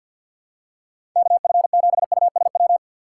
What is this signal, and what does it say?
Morse code, heard as a keyed tone, about 52 wpm, 700 Hz. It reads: KF7USW
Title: USA Call Signs - 6 Characters - 6x Character Spacing - 50wpm